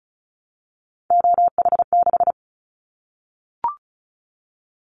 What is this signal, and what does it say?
Morse code, heard as a keyed tone, about 34 wpm, 700 Hz. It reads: OH6
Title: Callsign Prefixes for Top 25 DXCC - Repeated 3x - 35wpm